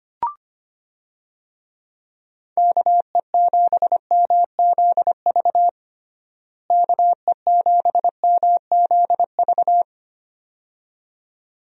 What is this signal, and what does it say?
Morse code, heard as a keyed tone, about 25 wpm, 700 Hz. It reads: KE7MZV KE7MZV
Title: USA Call Signs - 6 Characters - Repeated 2x - 25wpm